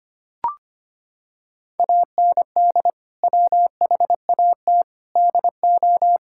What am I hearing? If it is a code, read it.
Morse code, heard as a keyed tone, about 25 wpm, 700 Hz. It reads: AND WHAT DO
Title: Sentences from Top 1600 Words 25wpm